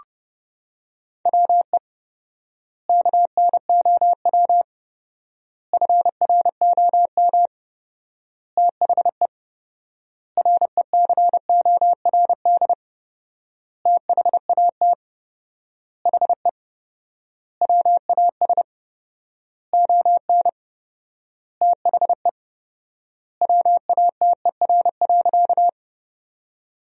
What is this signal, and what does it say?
Morse code, heard as a keyed tone, about 30 wpm, 700 Hz. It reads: WE KNOW FROM THE RECORD THAT HE WAS ON THE WATER.
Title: 4x Word-Spacing - Sentences from Top 500 Words - 30wpm